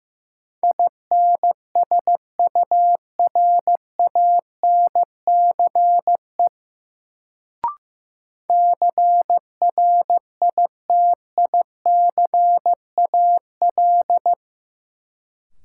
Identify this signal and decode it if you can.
Morse code, heard as a keyed tone, about 15 wpm, 700 Hz. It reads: INSURANCE CRITICAL
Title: Sentences from Top 1100 Words - New Word Drill - 15wpm